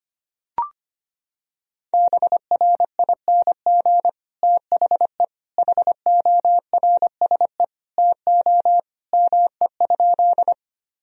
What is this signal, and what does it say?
Morse code, heard as a keyed tone, about 25 wpm, 700 Hz. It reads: BRING THE HORSE TO ME?
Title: Sentences from Top 1000 Words - Review Part 3 - 25wpm